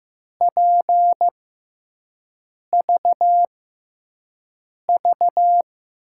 Morse code, heard as a keyed tone, about 15 wpm, spaced as for 5 wpm, 700 Hz. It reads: PVV
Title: USA Call Signs - 6 Characters - 6x Character Spacing - 15wpm